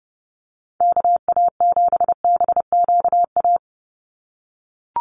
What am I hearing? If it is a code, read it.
Morse code, heard as a keyed tone, about 30 wpm, 700 Hz. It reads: KA7BQA
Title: USA Call Signs 30wpm